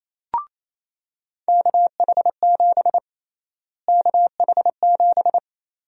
Morse code, heard as a keyed tone, about 28 wpm, 700 Hz. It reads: KH7 KH7
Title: USA Call Sign Prefixes - Repeated 2x - 28wpm